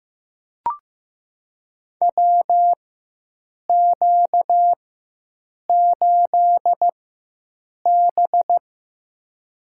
Morse code, heard as a keyed tone, about 15 wpm, spaced as for 7 wpm, 700 Hz. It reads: WQ8B
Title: USA Call Signs - 4x Character Spacing - 15wpm